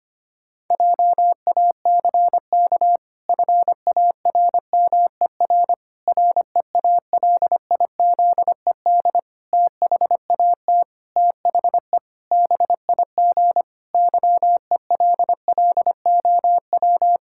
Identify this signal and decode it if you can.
Morse code, heard as a keyed tone, about 25 wpm, 700 Hz. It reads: JACK FARMER REALIZED THAT THE BIG YELLOW
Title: Pangram Challenge v2 - 25wpm